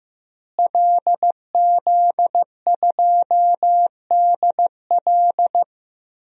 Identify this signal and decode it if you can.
Morse code, heard as a keyed tone, about 15 wpm, 700 Hz. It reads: LZ2DL
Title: DXCC-100 Call Signs - 5 Characters - Repeated 3x - 15wpm